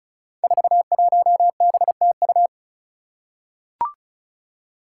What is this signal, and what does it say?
Morse code, heard as a keyed tone, about 36 wpm, 700 Hz. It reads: 41BTU
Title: Five Challenging Letter-Number Combinations 35wpm